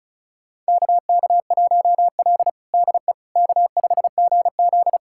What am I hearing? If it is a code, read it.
Morse code, heard as a keyed tone, about 36 wpm, 700 Hz. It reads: KK1L DE K5GZ
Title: USA Callsign DE Callsign - 4 Characters - 35wpm